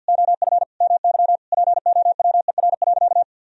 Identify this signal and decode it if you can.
Morse code, heard as a keyed tone, about 50 wpm, 700 Hz. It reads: OF MY POWER.
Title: Sentences from Top 1000 Words - Review Part 4 - 50wpm